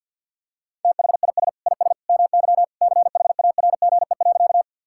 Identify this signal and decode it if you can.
Morse code, heard as a keyed tone, about 50 wpm, 700 Hz. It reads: THIS IS MY CHARGE.
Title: Sentences from Top 1000 Words - Review Part 6 - 50wpm